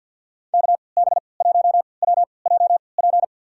Morse code, heard as a keyed tone, about 53 wpm, spaced as for 29 wpm, 700 Hz. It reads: KB1WJP
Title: USA Call Signs - 3x Character Spacing - 50wpm